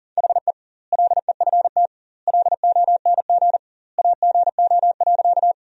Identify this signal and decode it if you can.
Morse code, heard as a keyed tone, about 40 wpm, 700 Hz. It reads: HE LEFT LONG AGO.
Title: 2x Word-Spacing - Sentences from Top 500 Words - 40wpm